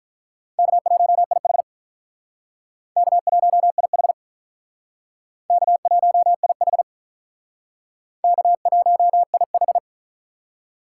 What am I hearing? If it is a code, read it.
Morse code, heard as a keyed tone, about 47 wpm, 700 Hz. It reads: K1IH K1IH K1IH K1IH